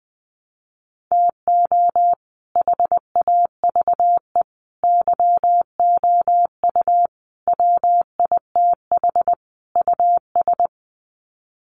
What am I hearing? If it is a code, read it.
Morse code, heard as a keyed tone, about 20 wpm, 700 Hz. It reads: TO HAVE YOU WITH US
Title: Sets of 5 Words 20wpm